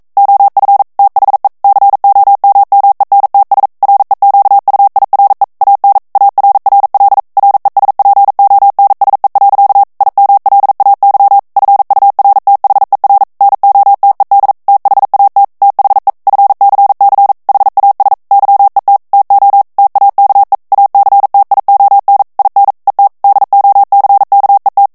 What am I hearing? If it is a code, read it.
Morse code, heard as a keyed tone, about 42 wpm, 800 Hz. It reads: OF THE COMMENTS REQUIRE AN ARRL RESPONSE. IMLAY FURTHER NOTED THAT THE FCC HAS YET TO TAKE ACTION IN ET DOCKET